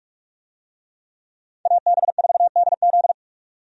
Morse code, heard as a keyed tone, about 46 wpm, 700 Hz. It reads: AB4DZ
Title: USA Call Signs - 5 Characters - 45wpm